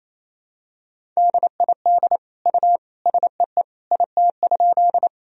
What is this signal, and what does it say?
Morse code, heard as a keyed tone, about 28 wpm, 700 Hz. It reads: DID U SEE IT?
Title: Sentences from Top 100 Words and Abbreviations 28wpm